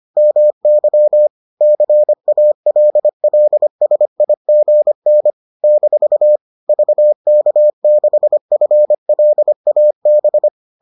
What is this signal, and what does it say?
Morse code, heard as a keyed tone, about 25 wpm, 600 Hz. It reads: MY CALLSIGN - VK6FLAB